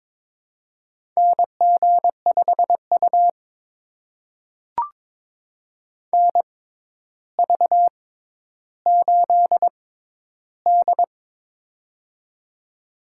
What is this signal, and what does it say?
Morse code, heard as a keyed tone, about 23 wpm, spaced as for 11 wpm, 700 Hz. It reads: NG5U NV8D
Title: USA Call Signs - 4 Characters - 6x Character Spacing - 22wpm